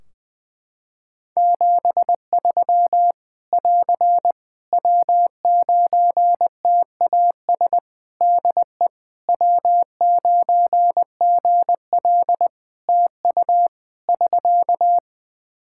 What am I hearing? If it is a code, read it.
Morse code, heard as a keyed tone, about 21 wpm, 700 Hz. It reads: 73 <AR> W9TAS DE W9GL TU <SK>
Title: Practice QSO - Exchange 4 - Ending - 20wpm